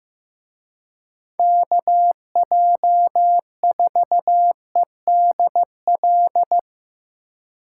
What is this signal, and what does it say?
Morse code, heard as a keyed tone, about 17 wpm, 700 Hz. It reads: KJ4EDL
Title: Speed Racing - USA 2x3 Call Signs 15wpm